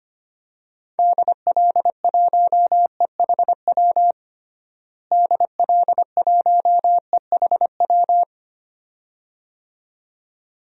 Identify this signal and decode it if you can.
Morse code, heard as a keyed tone, about 25 wpm, 700 Hz. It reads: DL1EHW DL1EHW
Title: DXCC-100 Call Signs - 6 Characters - Repeated 2x - 25wpm